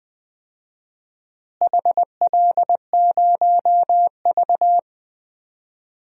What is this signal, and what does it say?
Morse code, heard as a keyed tone, about 21 wpm, 700 Hz. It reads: HL0V
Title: DXCC-100 Call Signs - 4 Characters - Repeated 3x - 20wpm